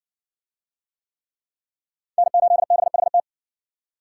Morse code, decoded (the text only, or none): N8BHT